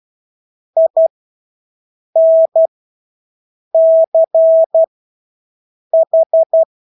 Morse code, heard as a keyed tone, about 12 wpm, spaced as for 6 wpm, 650 Hz. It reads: INCH
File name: Engelske ord 12-6 wpm